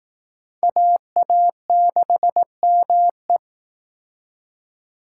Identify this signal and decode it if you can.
Morse code, heard as a keyed tone, about 19 wpm, 700 Hz. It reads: AA6ME